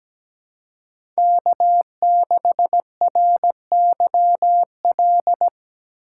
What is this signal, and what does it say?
Morse code, heard as a keyed tone, about 17 wpm, 700 Hz. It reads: K6RYL
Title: USA Call Signs - 5 Characters - 3x Character Spacing - 17wpm